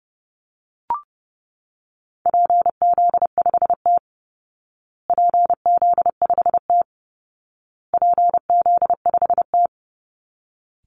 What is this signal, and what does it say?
Morse code, heard as a keyed tone, about 30 wpm, 700 Hz. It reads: PZ5T PZ5T PZ5T
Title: Call Signs Repeated 2x - 30wpm